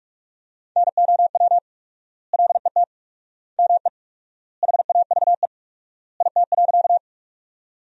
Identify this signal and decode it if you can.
Morse code, heard as a keyed tone, about 44 wpm, 700 Hz. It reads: NOW LET ME HAVE IT.
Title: 4x Word-Spacing - Sentences from Top 300 Words - 45wpm